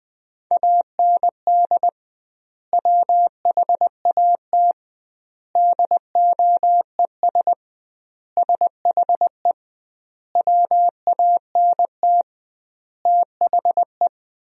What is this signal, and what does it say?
Morse code, heard as a keyed tone, about 20 wpm, 700 Hz. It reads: AND WHAT DOES SHE WANT THE
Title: 2x Word-Spacing - Sentences from Top 400 Words - 20wpm